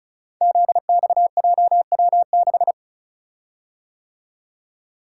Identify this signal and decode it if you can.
Morse code, heard as a keyed tone, about 36 wpm, 700 Hz. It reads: ZXJW6
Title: Code Groups - 5 Letter:Numbers - 35wpm